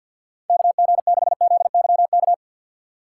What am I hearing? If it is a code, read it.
Morse code, heard as a keyed tone, about 49 wpm, 700 Hz. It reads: KG6ZYK
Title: USA Call Signs - 6 Characters - 3x Character Spacing - 50wpm